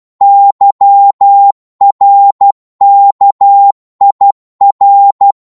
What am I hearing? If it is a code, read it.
Morse code, heard as a keyed tone, about 12 wpm, 800 Hz. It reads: YRKIR
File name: Morse / ltr-12wpm